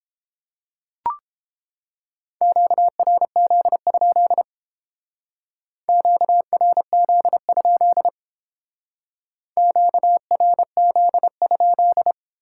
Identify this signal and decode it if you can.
Morse code, heard as a keyed tone, about 29 wpm, 700 Hz. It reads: QRZ? QRZ? QRZ?